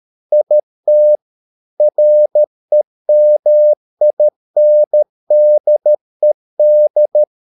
Morse code, heard as a keyed tone, about 13 wpm, 600 Hz. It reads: IT REMINDED